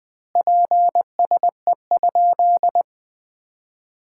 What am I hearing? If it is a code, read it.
Morse code, heard as a keyed tone, about 20 wpm, 700 Hz. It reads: PSE?
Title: Common QSO Phrases - 4x Word-Spacing - 20wpm